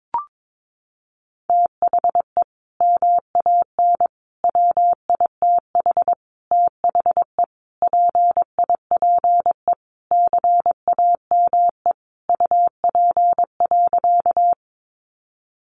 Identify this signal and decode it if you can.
Morse code, heard as a keyed tone, about 22 wpm, 700 Hz. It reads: THE MAN WITH THE PIPE CAME UP.
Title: Sentences from Top 1500 Words 22wpm